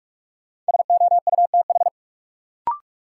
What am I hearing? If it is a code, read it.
Morse code, heard as a keyed tone, about 46 wpm, 700 Hz. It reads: SOUTH
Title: Sentences from Top 400 Words - New Word Drill - 45wpm